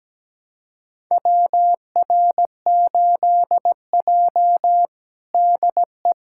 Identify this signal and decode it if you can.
Morse code, heard as a keyed tone, about 17 wpm, 700 Hz. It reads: WR8J DE
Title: USA Callsign DE Callsign - 4 Characters - 17wpm